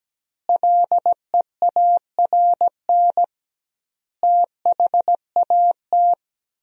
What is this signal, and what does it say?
Morse code, heard as a keyed tone, about 17 wpm, 700 Hz. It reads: LEARN THAT
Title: 2x Word-Spacing - Sentences from Top 300 Words - 17wpm